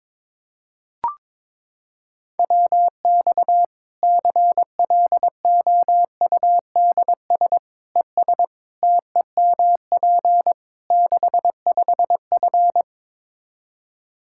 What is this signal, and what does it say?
Morse code, heard as a keyed tone, about 22 wpm, 700 Hz. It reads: WX CLOUDS ES TEMP 65F
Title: Practice QSO - Exchange 2 - Info - 22wpm